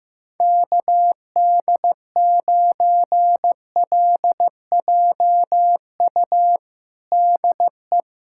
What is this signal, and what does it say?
Morse code, heard as a keyed tone, about 15 wpm, 700 Hz. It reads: KD9LJU DE
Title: USA Callsign DE Callsign 15wpm